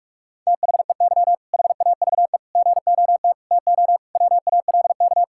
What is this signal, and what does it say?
Morse code, heard as a keyed tone, about 46 wpm, 700 Hz. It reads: THEY HAVE GOT TO WALK
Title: Sentences from Top 1000 Words - Review Part 6 - 45wpm